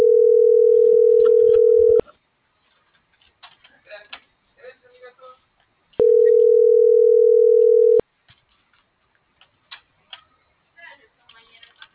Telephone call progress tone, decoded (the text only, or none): ringback tone